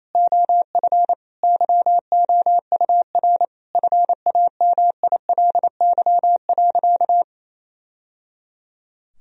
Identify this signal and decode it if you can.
Morse code, heard as a keyed tone, about 28 wpm, 700 Hz. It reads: OF YOUR FAMILY.
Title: Sentences from Top 1000 Words - Review Part 4 - 28wpm